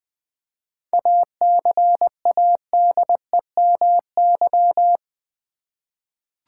Morse code, heard as a keyed tone, about 20 wpm, 700 Hz. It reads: ACADEMY
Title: 7 Letter Words - First 3 Letters Given - 20wpm